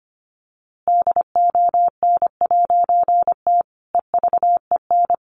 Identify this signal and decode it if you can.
Morse code, heard as a keyed tone, about 25 wpm, 700 Hz. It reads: DON'T EVEN
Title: A Sound Of Thunder 25wpm